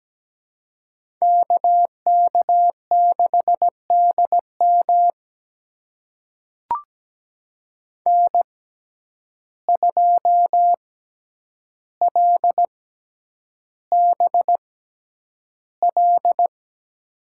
Morse code, heard as a keyed tone, about 17 wpm, spaced as for 9 wpm, 700 Hz. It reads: KK6DM N2LBL
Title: USA Call Signs - 5 Characters - 6x Character Spacing - 17wpm